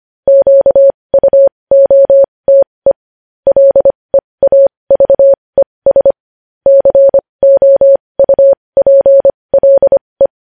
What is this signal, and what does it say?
Morse code, heard as a keyed tone, about 25 wpm, spaced as for 20 wpm, 550 Hz. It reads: QUOTE LEAVES COUPLE
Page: news-headlines-in-morse-code-at-20-wpm